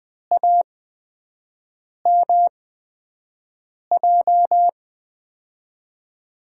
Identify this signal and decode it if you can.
Morse code, heard as a keyed tone, about 21 wpm, spaced as for 5 wpm, 700 Hz. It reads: AMJ
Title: USA Call Signs - 6 Characters - 8x Character Spacing - 20wpm